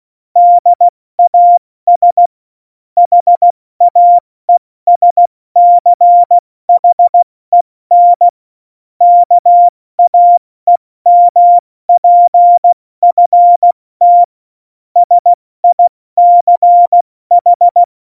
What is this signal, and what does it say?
Morse code, heard as a keyed tone, about 16 wpm, 700 Hz. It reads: DAS HAESCHEN KAEMPFT SICH
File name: Morsekurs